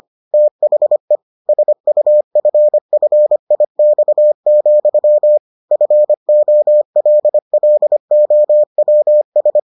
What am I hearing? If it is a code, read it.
Morse code, heard as a keyed tone, about 25 wpm, 600 Hz. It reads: THE SUFFIX, FOLLOWS